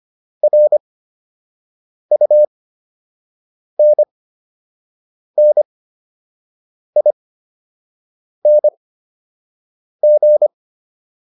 Morse code, heard as a keyed tone, about 25 wpm, spaced as for 6 wpm, 600 Hz. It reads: RUNNING